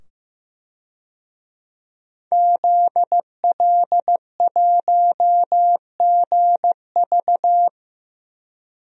Morse code, heard as a keyed tone, about 15 wpm, 700 Hz. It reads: ZL1GV
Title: DXCC-100 Call Signs - 5 Characters - 15wpm